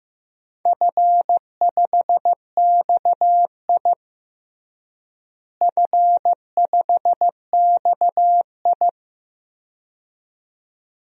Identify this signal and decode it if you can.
Morse code, heard as a keyed tone, about 16 wpm, 700 Hz. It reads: F5XI F5XI
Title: DXCC-100 Call Signs - 4 Characters - Repeated 2x - 15wpm